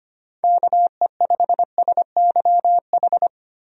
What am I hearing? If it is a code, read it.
Morse code, heard as a keyed tone, about 25 wpm, 700 Hz. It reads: KE5SYH
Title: USA Call Signs - 6 Characters - Repeated 2x - 25wpm